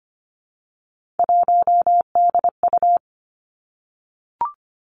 Morse code, heard as a keyed tone, about 25 wpm, 700 Hz. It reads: 1DU
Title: Three Challenging Letter-Number Combinations 25wpm